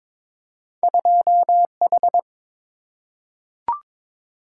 Morse code, heard as a keyed tone, about 22 wpm, 700 Hz. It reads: 2H
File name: practice / 22wpm / Two Letter-Number Combinations - 22wpm